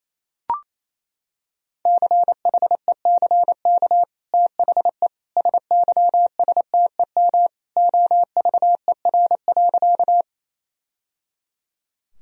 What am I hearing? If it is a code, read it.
Morse code, heard as a keyed tone, about 28 wpm, 700 Hz. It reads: CHECK THE SYSTEM OVER.
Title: Sentences from Top 1000 Words - Review Part 3 - 28wpm